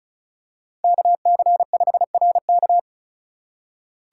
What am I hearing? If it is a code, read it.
Morse code, heard as a keyed tone, about 35 wpm, 700 Hz. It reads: KC5RK
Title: USA Call Signs - 5 Characters - 3x Character Spacing - 35wpm